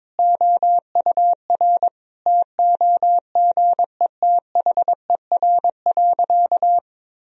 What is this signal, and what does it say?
Morse code, heard as a keyed tone, about 21 wpm, 700 Hz. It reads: OUR TOGETHER.
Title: Sentences from Top 1000 Words - Review Part 1 - 22wpm